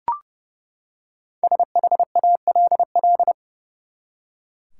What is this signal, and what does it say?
Morse code, heard as a keyed tone, about 30 wpm, 700 Hz. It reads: SHALL
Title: Five Letter Words - 30wpm